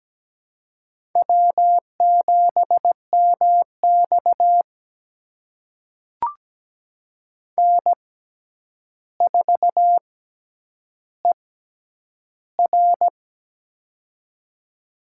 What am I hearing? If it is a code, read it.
Morse code, heard as a keyed tone, about 17 wpm, spaced as for 9 wpm, 700 Hz. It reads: W7MX N4ER
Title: USA Call Signs - 4 Characters - 6x Character Spacing - 17wpm